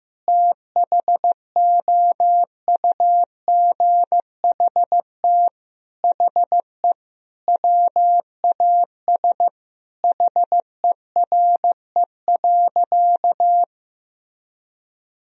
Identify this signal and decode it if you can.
Morse code, heard as a keyed tone, about 15 wpm, 700 Hz. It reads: THOUGHT HE WAS HERE.
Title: Sentences from Top 1000 Words - Review Part 3 - 15wpm